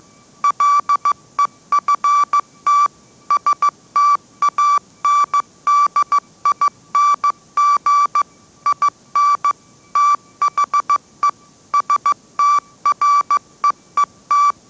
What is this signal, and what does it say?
Morse code, heard as a keyed tone, about 15 wpm, 1160 Hz. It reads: LEFT STANDING IN THE STREET
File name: mrx / mrx066 16wpm